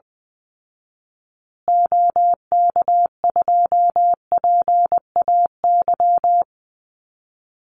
Morse code, heard as a keyed tone, about 20 wpm, 700 Hz. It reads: OK2PAY
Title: Call Signs Repeated 3x - 20wpm